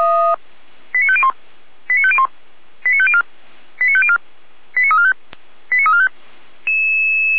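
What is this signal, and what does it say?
Five-tone selective call (ZVEI), heard as quick runs of five tones, tones of about 70 ms. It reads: 89501 89501 89504 89504 89345 89345
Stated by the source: ZVEI five-tone selective call received on an SDR.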